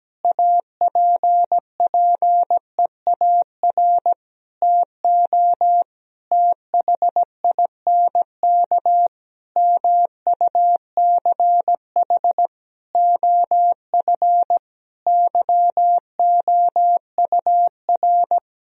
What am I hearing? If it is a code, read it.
Morse code, heard as a keyed tone, about 17 wpm, 700 Hz. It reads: APPEAR TO THINK MUCH OF YOUR
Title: Sentences from Top 1000 Words - Review Part 2 - 17wpm